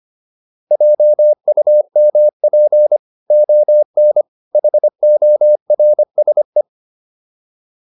Morse code, heard as a keyed tone, about 25 wpm, 600 Hz. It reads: JUMP ON HORSE
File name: PR405 25